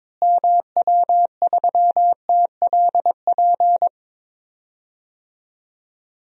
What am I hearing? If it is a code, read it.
Morse code, heard as a keyed tone, about 22 wpm, 700 Hz. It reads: MW3TLP
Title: DXCC-100 Call Signs - 6 Characters - Repeated 2x - 22wpm